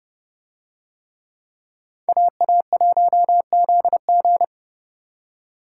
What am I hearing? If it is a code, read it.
Morse code, heard as a keyed tone, about 29 wpm, 700 Hz. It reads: AA1ZG